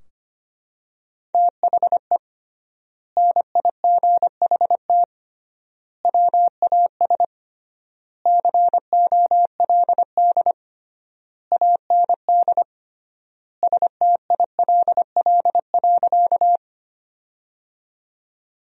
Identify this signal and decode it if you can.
Morse code, heard as a keyed tone, about 25 wpm, 700 Hz. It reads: THE NIGHT WAS COLD AND STILL.
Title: 3x Word-Spacing - Sentences from Top 400 Words - 25wpm